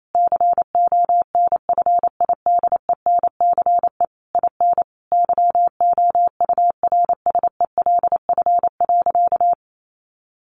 Morse code, heard as a keyed tone, about 28 wpm, 700 Hz. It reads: CONFIDENCE IN YOURSELF.
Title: Sentences from Top 1500 Words 28wpm